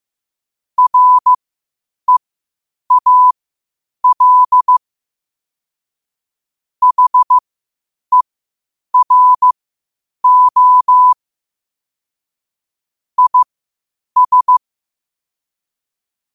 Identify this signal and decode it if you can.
Morse code, heard as a keyed tone, about 15 wpm, spaced as for 8 wpm, 1000 Hz. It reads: REAL HERO IS
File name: qotdfiles / 8wpm